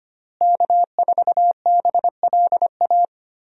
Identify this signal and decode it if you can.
Morse code, heard as a keyed tone, about 25 wpm, 700 Hz. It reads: K4BLA
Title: USA Call Signs - 5 Characters - Repeated 3x - 25wpm